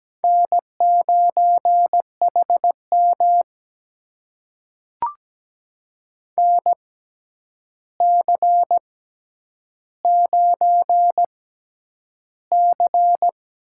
Morse code, heard as a keyed tone, about 17 wpm, spaced as for 9 wpm, 700 Hz. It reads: N9HM NC9C
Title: USA Call Signs - 6x Character Spacing - 17wpm